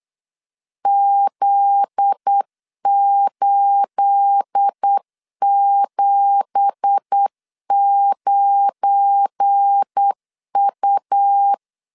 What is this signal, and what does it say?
Morse code, heard as a keyed tone, about 8 wpm, 800 Hz. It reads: Z879U
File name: LEZIONI / 10 SIGNS / 10A SLASH 8wpm 3452